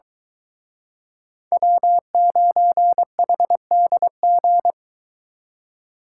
Morse code, heard as a keyed tone, about 23 wpm, 700 Hz. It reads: W9HDG